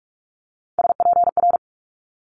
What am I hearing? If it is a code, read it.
Morse code, heard as a keyed tone, about 46 wpm, 700 Hz. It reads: SPR
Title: Multi-letter Phonemes Quick Fire 45wpm